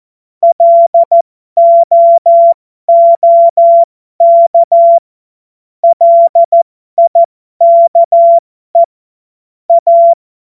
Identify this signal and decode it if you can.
Morse code, heard as a keyed tone, about 14 wpm, spaced as for 12 wpm, 680 Hz. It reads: LOOK LIKE A
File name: Morsekurs / Witze